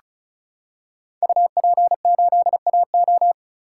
Morse code, heard as a keyed tone, about 36 wpm, 700 Hz. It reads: UP8AO
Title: Code Groups - 5 Letter:Numbers - 35wpm